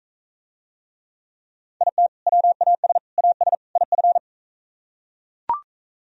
Morse code, heard as a keyed tone, about 42 wpm, 700 Hz. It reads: IT WAS AS IF